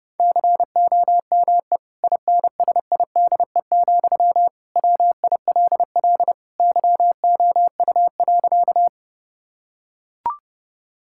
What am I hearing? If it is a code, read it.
Morse code, heard as a keyed tone, about 30 wpm, 700 Hz. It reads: COME INSIDE, WILL YOU.
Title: Sentences from Top 1000 Words - Review Part 6 - 30wpm